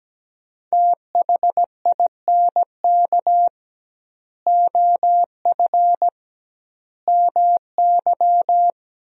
Morse code, heard as a keyed tone, about 17 wpm, 700 Hz. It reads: THINK OF MY
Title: 2x Word-Spacing - Sentences from Top 300 Words - 17wpm